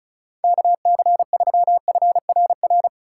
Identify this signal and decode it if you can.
Morse code, heard as a keyed tone, about 36 wpm, 700 Hz. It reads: KC3FRR
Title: USA Call Signs - 3x Character Spacing - 35wpm